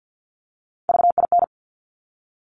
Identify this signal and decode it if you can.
Morse code, heard as a keyed tone, about 50 wpm, 700 Hz. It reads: VIN
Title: Top Three Letter Combinations in Words 50wpm